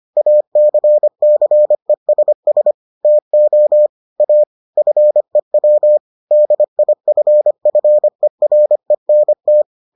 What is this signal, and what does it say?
Morse code, heard as a keyed tone, about 25 wpm, 600 Hz. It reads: ACCESS TO A FEW DIFFERENT